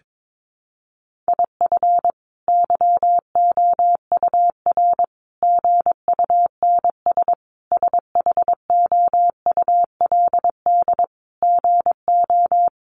Morse code, heard as a keyed tone, about 22 wpm, 700 Hz. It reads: IF YOUR GUNS SHOULD GO
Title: A Sound Of Thunder 22wpm